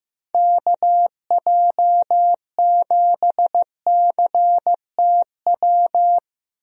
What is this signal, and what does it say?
Morse code, heard as a keyed tone, about 15 wpm, 700 Hz. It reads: KJ7CTW
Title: USA Call Signs - 6 Characters - Repeated 3x - 15wpm